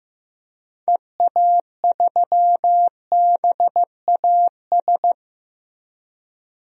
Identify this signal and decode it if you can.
Morse code, heard as a keyed tone, about 15 wpm, 700 Hz. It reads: EA3BAS
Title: DXCC-100 Call Signs - 6 Characters - 15wpm